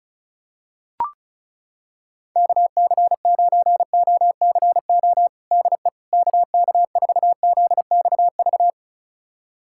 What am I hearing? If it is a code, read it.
Morse code, heard as a keyed tone, about 36 wpm, 700 Hz. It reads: KC9OCO DE KK4ZXV
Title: USA Callsign DE Callsign 35wpm